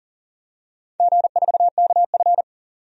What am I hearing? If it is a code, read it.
Morse code, heard as a keyed tone, about 41 wpm, 700 Hz. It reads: G4KF
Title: DXCC-100 Call Signs - 4 Characters - 40wpm